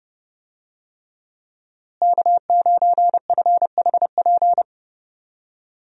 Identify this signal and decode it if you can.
Morse code, heard as a keyed tone, about 30 wpm, 700 Hz. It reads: K9FHP
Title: USA Call Signs - 5 Characters - Repeated 3x - 30wpm